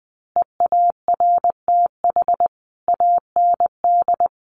Morse code, heard as a keyed tone, about 20 wpm, 700 Hz. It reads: EARTH AND
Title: A Sound Of Thunder 20wpm